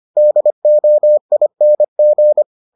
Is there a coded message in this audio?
Yes: Morse code from 0.2 to 2.4 s, heard as a keyed tone, about 25 wpm, 600 Hz. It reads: DOING